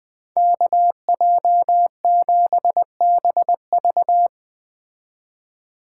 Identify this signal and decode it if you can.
Morse code, heard as a keyed tone, about 19 wpm, 700 Hz. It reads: KJ7BV
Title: Speed Racing - USA 2x2 Call Signs 17wpm